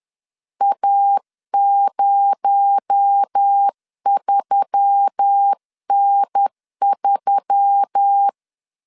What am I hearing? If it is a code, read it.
Morse code, heard as a keyed tone, about 10 wpm, 800 Hz. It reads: A03N3
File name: CW / DET / 1 6 1500C S C 10wpm 4252